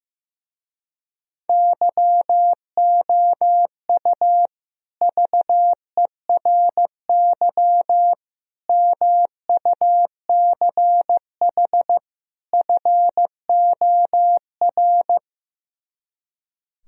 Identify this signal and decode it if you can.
Morse code, heard as a keyed tone, about 15 wpm, 700 Hz. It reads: YOU VERY MUCH FOR
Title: Speed-Racing - Sets of 4 Words 15wpm